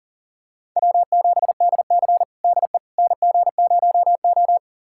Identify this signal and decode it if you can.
Morse code, heard as a keyed tone, about 40 wpm, 700 Hz. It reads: W7DC DE NG0O
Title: USA Callsign DE Callsign - 4 Characters - 40wpm